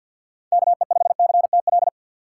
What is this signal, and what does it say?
Morse code, heard as a keyed tone, about 53 wpm, 700 Hz. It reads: KE5CTL
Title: USA Call Signs - 6 Characters - 3x Character Spacing - 50wpm